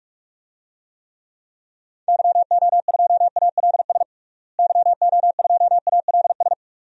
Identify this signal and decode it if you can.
Morse code, heard as a keyed tone, about 41 wpm, 700 Hz. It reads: YO2ALS YO2ALS
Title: DXCC-100 Call Signs - 6 Characters - Repeated 3x - 45wpm